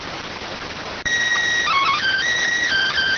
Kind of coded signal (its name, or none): SSTV (slow-scan television)